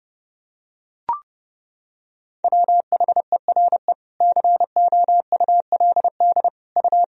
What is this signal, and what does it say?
Morse code, heard as a keyed tone, about 31 wpm, 700 Hz. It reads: WHERE COULD U
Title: Sentences from Top 300 Words and Abbreviations 30wpm